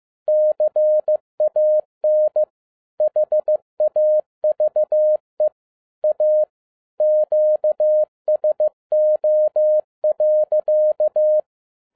Morse code, heard as a keyed tone, about 15 wpm, 610 Hz. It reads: CAN HAVE A QSO.